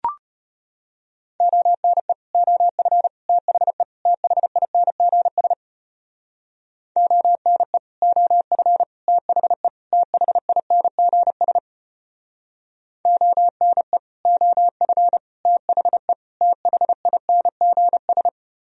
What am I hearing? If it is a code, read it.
Morse code, heard as a keyed tone, about 31 wpm, 700 Hz. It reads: ONE OF THE THINGS ONE OF THE THINGS ONE OF THE THINGS